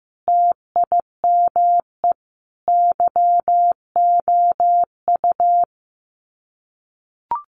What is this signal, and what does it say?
Morse code, heard as a keyed tone, about 15 wpm, 700 Hz. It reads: TIME YOU
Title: Sets of 5 Words Encore2 15wpm